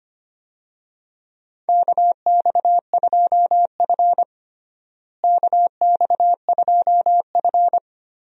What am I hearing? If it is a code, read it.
Morse code, heard as a keyed tone, about 25 wpm, 700 Hz. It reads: KX2F KX2F
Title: USA Call Signs - 4 Characters - Repeated 2x - 25wpm